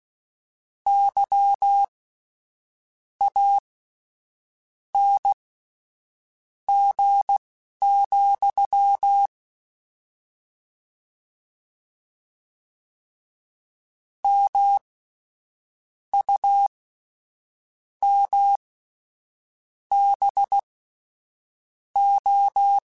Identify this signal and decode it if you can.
Morse code, heard as a keyed tone, about 16 wpm, spaced as for 6 wpm, 800 Hz. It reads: YANG, MUMBO
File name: Misc / Misc-mp3s / WordPairs sp @10